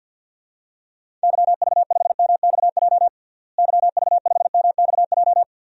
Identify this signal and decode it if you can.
Morse code, heard as a keyed tone, about 52 wpm, 700 Hz. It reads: YV5MXJ YV5MXJ
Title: DXCC-100 Call Signs - 6 Characters - Repeated 2x - 50wpm